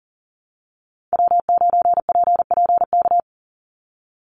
Morse code, heard as a keyed tone, about 40 wpm, 700 Hz. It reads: W9PPK
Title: USA Call Signs 40wpm